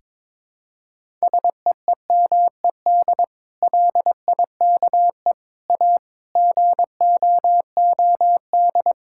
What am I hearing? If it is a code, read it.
Morse code, heard as a keyed tone, about 22 wpm, 700 Hz. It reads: SEEMED LIKE A GOOD
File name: practice / 22wpm / Sets of 8 Words - 22wpm